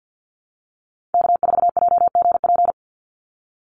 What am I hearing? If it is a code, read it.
Morse code, heard as a keyed tone, about 48 wpm, 700 Hz. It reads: K4JGP
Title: USA Call Signs 50wpm